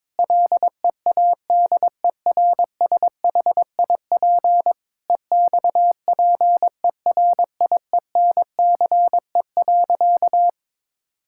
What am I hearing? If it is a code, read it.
Morse code, heard as a keyed tone, about 22 wpm, 700 Hz. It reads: LEADERSHIP EXPERIENCE.
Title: Sentences from Top 1600 Words 22wpm